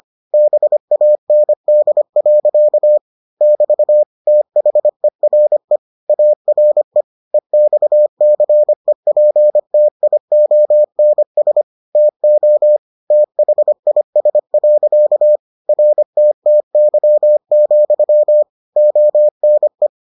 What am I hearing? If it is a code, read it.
Morse code, heard as a keyed tone, about 25 wpm, 600 Hz. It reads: BAND. <BT> THERE ARE EXCEPTIONS TO THIS. RTTY, ONE